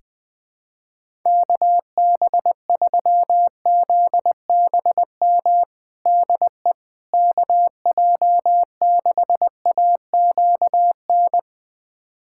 Morse code, heard as a keyed tone, about 20 wpm, 700 Hz. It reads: KB3ZBM DE KJ6AQN
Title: USA Callsign DE Callsign - 6 Characters - 20wpm